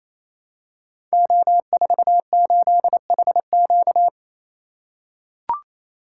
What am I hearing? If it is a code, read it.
Morse code, heard as a keyed tone, about 29 wpm, 700 Hz. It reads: O48HQ
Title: Code Groups - 5 Letter:Numbers - 28wpm